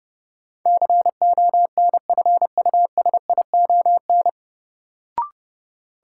Morse code, heard as a keyed tone, about 30 wpm, 700 Hz. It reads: CONFUSION
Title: Sentences from Top 1600 Words - New Word Drill - 30wpm